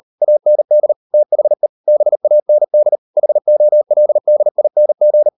Morse code, heard as a keyed tone, about 40 wpm, 600 Hz. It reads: AND THE BAND HOLDING